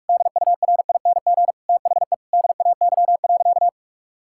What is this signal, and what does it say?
Morse code, heard as a keyed tone, about 46 wpm, 700 Hz. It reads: DURING THE DAY.
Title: Sentences from Top 1000 Words - Review Part 6 - 45wpm